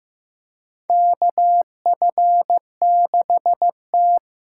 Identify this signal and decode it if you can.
Morse code, heard as a keyed tone, about 16 wpm, 700 Hz. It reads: KF6T
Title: K1USN SST QSO Practice - 15wpm